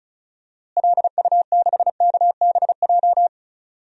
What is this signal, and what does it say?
Morse code, heard as a keyed tone, about 34 wpm, 700 Hz. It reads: LU6KBJ
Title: DXCC-100 Call Signs - 6 Characters - Repeated 3x - 35wpm